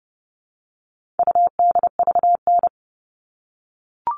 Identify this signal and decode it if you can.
Morse code, heard as a keyed tone, about 28 wpm, 700 Hz. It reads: UDVN
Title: All Cut Numbers - 4-Digit 30wpm